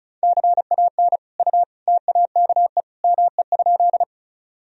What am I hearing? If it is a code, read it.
Morse code, heard as a keyed tone, about 34 wpm, 700 Hz. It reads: CAN U TAKE ME?
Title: Sentences from Top 200 Words and Abbreviations 35wpm